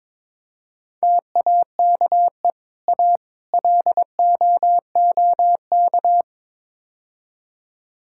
Morse code, heard as a keyed tone, about 22 wpm, 700 Hz. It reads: TAKE A LOOK
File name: practice / 22wpm / Build A Phrase - 5-Word Phrases - 22wpm